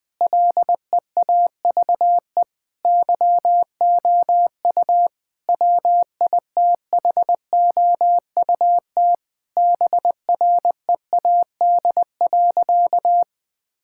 Morse code, heard as a keyed tone, about 20 wpm, 700 Hz. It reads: LEAVE YOU WITHOUT BREAD.
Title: Sentences from Top 1600 Words 20wpm